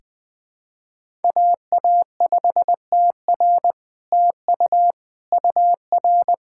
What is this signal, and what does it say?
Morse code, heard as a keyed tone, about 21 wpm, 700 Hz. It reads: AA5TR TU UR
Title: SOTA QSO Practice 20wpm